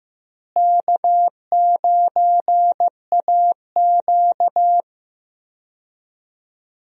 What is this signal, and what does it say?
Morse code, heard as a keyed tone, about 15 wpm, 700 Hz. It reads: K9AQ
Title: USA Call Signs - 4 Characters - Repeated 2x - 15wpm